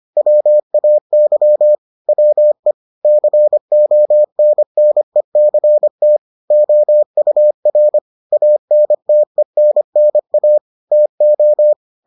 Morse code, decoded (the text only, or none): WAY WE CONNECT OUR ANTENNA TO